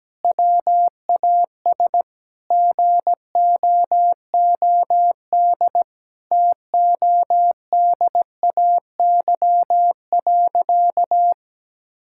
Morse code, heard as a keyed tone, about 17 wpm, 700 Hz. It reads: WAS GOOD TODAY.
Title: Sentences from Top 1000 Words - Review Part 5 - 17wpm